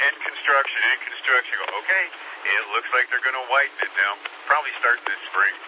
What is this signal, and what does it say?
Emergency Alert System (SAME) tones (header bursts), heard as harsh buzzing bursts, with speech. It is heard from 1.0 to 5.2 s.